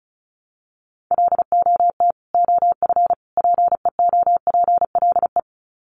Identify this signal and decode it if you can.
Morse code, heard as a keyed tone, about 34 wpm, 700 Hz. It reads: LOT OF PEOPLE
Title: Sets of 3 Words 35wpm